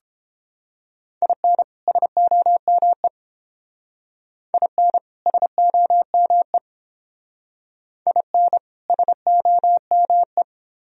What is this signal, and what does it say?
Morse code, heard as a keyed tone, about 30 wpm, 700 Hz. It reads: IN SOME IN SOME IN SOME